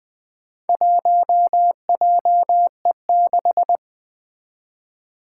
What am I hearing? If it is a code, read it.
Morse code, heard as a keyed tone, about 20 wpm, 700 Hz. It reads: 1JE6
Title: Four Challenging Letter-Number Combinations 20wpm